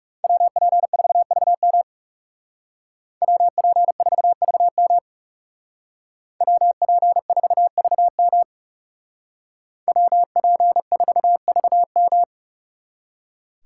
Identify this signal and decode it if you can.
Morse code, heard as a keyed tone, about 34 wpm, 700 Hz. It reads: WP4VM WP4VM WP4VM WP4VM
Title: Speed Racing - USA 2x2 Call Signs 30wpm